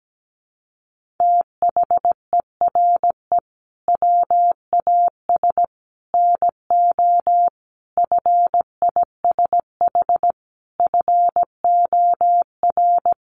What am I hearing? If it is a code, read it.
Morse code, heard as a keyed tone, about 17 wpm, 700 Hz. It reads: THERE WAS NO FISH FOR
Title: Sentences from Top 1500 Words 17wpm